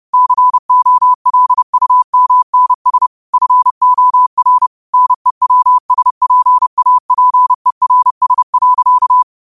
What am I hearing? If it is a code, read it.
Morse code, heard as a keyed tone, about 30 wpm, 1000 Hz. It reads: COLUMNS FOR NEWSPAPERS.
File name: 30wpm